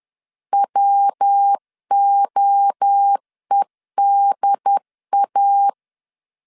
Morse code, heard as a keyed tone, about 10 wpm, 800 Hz. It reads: WOEDA
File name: CW / DET / 1 3 1500C S C 10wpm 3335